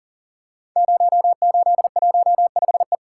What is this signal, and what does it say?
Morse code, heard as a keyed tone, about 42 wpm, 700 Hz. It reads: 0815E
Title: Code Groups - 5 Letter:Numbers - 40wpm